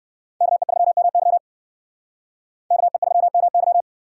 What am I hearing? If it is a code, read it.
Morse code, heard as a keyed tone, about 56 wpm, 700 Hz. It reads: KE3MY KE3MY